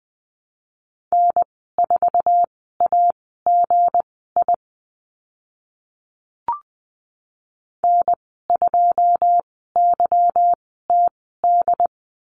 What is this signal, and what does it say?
Morse code, heard as a keyed tone, about 20 wpm, spaced as for 14 wpm, 700 Hz. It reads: N4AGI N2YTD
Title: 193 Lesson 31g - Letter v - Call Signs 20wpm